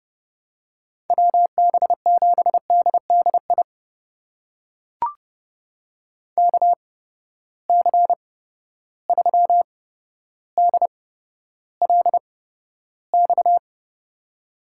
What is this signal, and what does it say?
Morse code, heard as a keyed tone, about 30 wpm, spaced as for 13 wpm, 700 Hz. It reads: WB7DDI KC3DLX
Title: USA Call Signs - 6 Characters - 8x Character Spacing - 30wpm